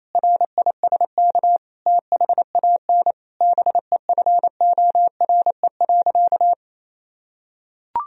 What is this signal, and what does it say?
Morse code, heard as a keyed tone, about 28 wpm, 700 Hz. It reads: RISK THAN BEFORE.
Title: Sentences from Top 1000 Words - Review Part 4 - 28wpm